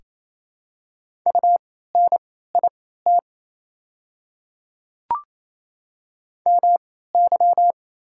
Morse code, heard as a keyed tone, about 28 wpm, spaced as for 16 wpm, 700 Hz. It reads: UNIT MY
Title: 3x Character Spacing - Top 500 Words - 28wpm